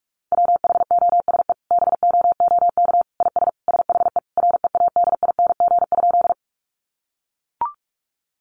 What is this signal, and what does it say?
Morse code, heard as a keyed tone, about 44 wpm, 700 Hz. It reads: WHOSE BOOK IS SHE READING?
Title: Sentences from Top 1200 Words 45wpm